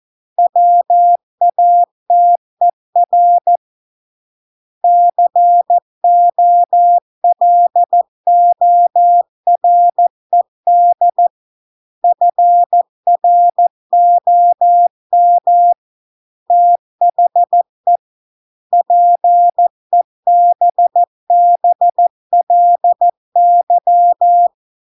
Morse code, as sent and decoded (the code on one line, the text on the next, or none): .-- .- - . .-. / -.-. --- .-.. --- .-. -.. / ..-. .-. --- -- / - .... / .--. -... -... .-.. -.--
WATER COLORED FROM THE PEBBLY